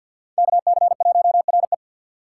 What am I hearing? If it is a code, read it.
Morse code, heard as a keyed tone, about 48 wpm, 700 Hz. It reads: KC1RE
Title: USA Call Signs - 5 Characters - 6x Character Spacing - 50wpm